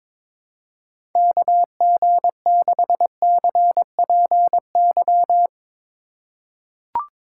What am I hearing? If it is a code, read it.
Morse code, heard as a keyed tone, about 21 wpm, 700 Hz. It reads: KG6CPY
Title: USA Call Signs - 2x Character Spacing - 22wpm